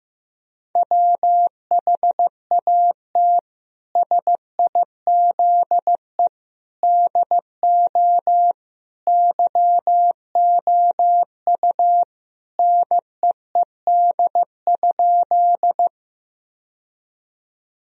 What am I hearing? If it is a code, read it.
Morse code, heard as a keyed tone, about 15 wpm, 700 Hz. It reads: WHAT SIZE DO YOU NEED?
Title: Sentences from Top 1000 Words - Review Part 2 - 15wpm